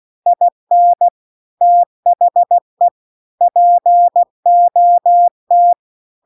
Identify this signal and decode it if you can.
Morse code, heard as a keyed tone, about 16 wpm, 700 Hz. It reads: IN THE POT